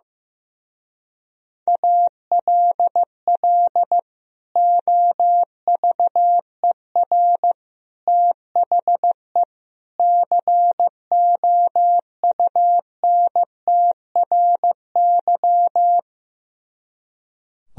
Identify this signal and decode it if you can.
Morse code, heard as a keyed tone, about 16 wpm, 700 Hz. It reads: ALL OVER THE COUNTRY
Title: Speed-Racing - Sets of 4 Words 15wpm